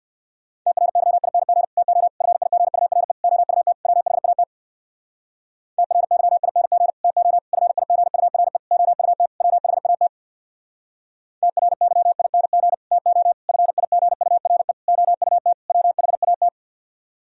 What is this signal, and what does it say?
Morse code, heard as a keyed tone, about 62 wpm, 700 Hz. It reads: TRYING TO FIGURE OUT WHAT TRYING TO FIGURE OUT WHAT TRYING TO FIGURE OUT WHAT
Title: Speed-Racing - Sets of 5 Words 50wpm